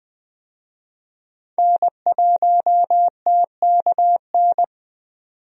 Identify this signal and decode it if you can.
Morse code, heard as a keyed tone, about 20 wpm, 700 Hz. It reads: N1TKN
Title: USA Call Signs - 5 Characters - Repeated 2x - 20wpm